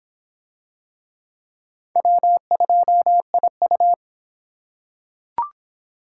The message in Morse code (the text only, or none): W2IU